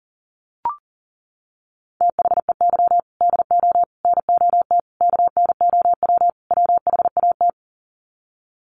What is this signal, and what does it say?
Morse code, heard as a keyed tone, about 40 wpm, 700 Hz. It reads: THEY DO NOT KNOW WHAT
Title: Sets of 5 Words Encore 40wpm